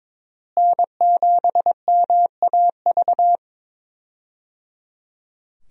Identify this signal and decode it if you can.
Morse code, heard as a keyed tone, about 22 wpm, 700 Hz. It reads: N7MAV
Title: USA Call Signs - 5 Characters - Repeated 3x - 22wpm